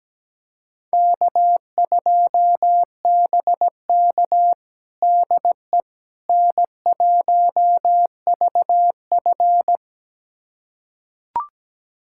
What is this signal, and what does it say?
Morse code, heard as a keyed tone, about 17 wpm, 700 Hz. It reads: K2BK DE N1VF
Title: USA Callsign DE Callsign - 4 Characters - 17wpm